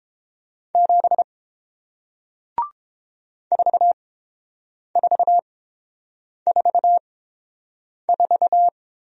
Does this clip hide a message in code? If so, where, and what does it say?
Yes: Morse code from 0.8 to 8.7 s, heard as a keyed tone, about 29 wpm, spaced as for 7 wpm, 700 Hz. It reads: 7 4444